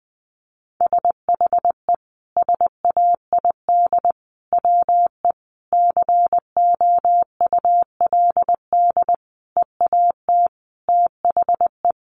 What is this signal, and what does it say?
Morse code, heard as a keyed tone, about 20 wpm, 700 Hz. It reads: SHE SAID WE COULD EAT THE
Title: Sentences from Top 1300 Words 20wpm